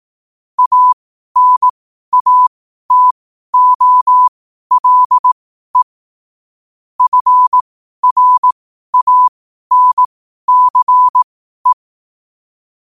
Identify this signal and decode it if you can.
Morse code, heard as a keyed tone, about 18 wpm, spaced as for 12 wpm, 1000 Hz. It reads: ANATOLE FRANCE
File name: qotdfiles / 12wpm